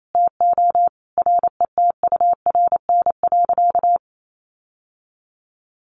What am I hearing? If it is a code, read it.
Morse code, heard as a keyed tone, about 28 wpm, 700 Hz. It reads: TO RETURN.
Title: Sentences from Top 800 Words 28wpm